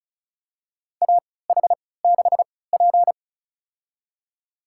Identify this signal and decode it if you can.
Morse code, heard as a keyed tone, about 36 wpm, spaced as for 20 wpm, 700 Hz. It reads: AH6P
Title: USA Call Signs - 4 Characters - 3x Character Spacing - 35wpm